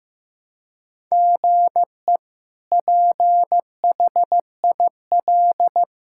Morse code, heard as a keyed tone, about 15 wpm, 700 Hz. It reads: GE PHIL
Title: K1USN SST QSO Practice - 15wpm